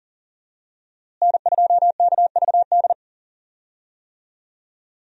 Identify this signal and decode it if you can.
Morse code, heard as a keyed tone, about 37 wpm, 700 Hz. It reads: N2KVD
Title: USA Call Signs - 5 Characters - Repeated 2x - 40wpm